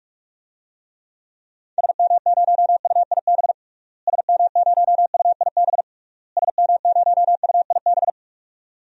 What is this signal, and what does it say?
Morse code, heard as a keyed tone, about 42 wpm, 700 Hz. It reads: SM0UIB SM0UIB SM0UIB
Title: DXCC-100 Call Signs - 6 Characters - Repeated 3x - 45wpm